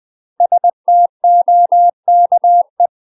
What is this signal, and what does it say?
Morse code, heard as a keyed tone, about 20 wpm, 700 Hz. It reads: STOKE
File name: UK MASTER QSOs TRAINING / 20WPM / UK MASTER QSOs 0071